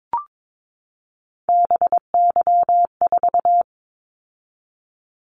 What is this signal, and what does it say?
Morse code, heard as a keyed tone, about 22 wpm, 700 Hz. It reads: BY4
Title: Callsign Prefixes for Top 100 DXCC - 22wpm